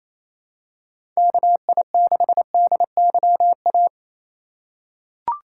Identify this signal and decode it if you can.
Morse code, heard as a keyed tone, about 28 wpm, 700 Hz. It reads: KI6DYA
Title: USA Call Signs - 6 Characters - 3x Character Spacing - 28wpm